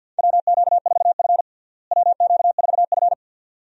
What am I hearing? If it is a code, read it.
Morse code, heard as a keyed tone, about 52 wpm, 700 Hz. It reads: WQ4F WQ4F
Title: USA Call Signs - 4 Characters - Repeated 2x - 50wpm